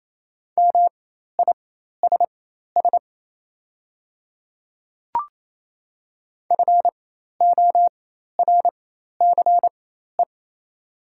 Morse code, heard as a keyed tone, about 27 wpm, spaced as for 13 wpm, 700 Hz. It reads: MISS FORCE
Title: 4x Character Spacing - Top 500 Words - 28wpm